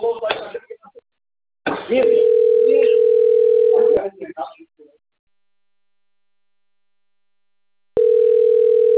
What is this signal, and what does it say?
Ringback tone (North America), 440 + 480 Hz, with speech.
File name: rg-1600-unknown- - - .93016